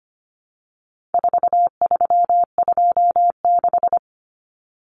Morse code, heard as a keyed tone, about 25 wpm, 700 Hz. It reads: 4326
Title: Common Cut Numbers - 4-Digit 25wpm